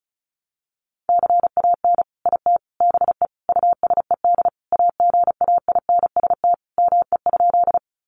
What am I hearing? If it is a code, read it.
Morse code, heard as a keyed tone, about 34 wpm, 700 Hz. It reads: CAN IT BE USED AGAINST ME?
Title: Sentences from Top 600 Words 35wpm